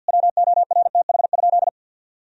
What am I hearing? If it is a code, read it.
Morse code, heard as a keyed tone, about 50 wpm, 700 Hz. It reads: WORTH?
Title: Sentences from Top 1000 Words - Review Part 4 - 50wpm